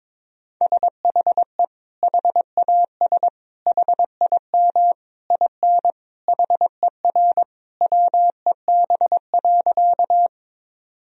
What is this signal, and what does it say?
Morse code, heard as a keyed tone, about 22 wpm, 700 Hz. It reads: SHE HAS HIM IN HER WEB.
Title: Sentences from Top 1000 Words - Review Part 4 - 22wpm